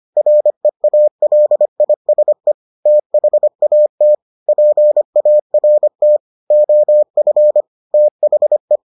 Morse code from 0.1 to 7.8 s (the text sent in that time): REALISE THAT PART OF